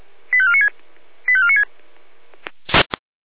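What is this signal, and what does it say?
Five-tone selective call (ZVEI), heard as quick runs of five tones, tones of about 70 ms. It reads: 85496 85496
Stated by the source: ZVEI five-tone selective call received on a software-defined radio.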